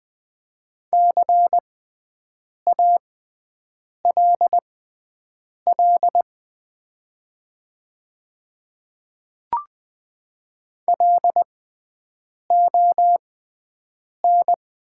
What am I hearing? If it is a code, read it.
Morse code, heard as a keyed tone, about 20 wpm, spaced as for 7 wpm, 690 Hz. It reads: CALL LON
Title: 6x Character Spacing - Top 100 Words - 20wpm